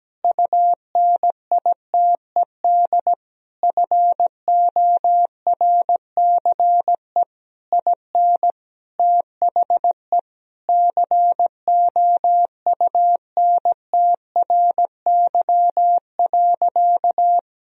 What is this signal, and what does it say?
Morse code, heard as a keyed tone, about 17 wpm, 700 Hz. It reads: UNITED FORCE IN THE COUNTRY.
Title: Sentences from Top 1000 Words - Review Part 4 - 17wpm